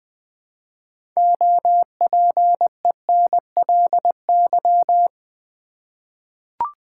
Morse code, heard as a keyed tone, about 20 wpm, 700 Hz. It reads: OPENLY
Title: 6 Letter Words - First 3 Letters Given - 20wpm